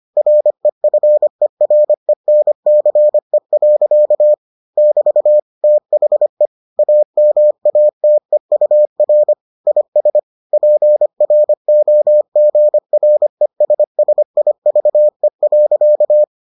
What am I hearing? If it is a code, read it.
Morse code, heard as a keyed tone, about 25 wpm, 600 Hz. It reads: REFERENCE. <BT> THE AMATEUR IS PROGRESSIVE.